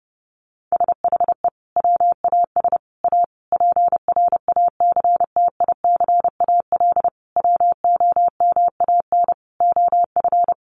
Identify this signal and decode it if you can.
Morse code, heard as a keyed tone, about 30 wpm, 700 Hz. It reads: SHE WAS A PRACTICAL WOMAN OF
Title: Sentences from Top 1400 Words 30wpm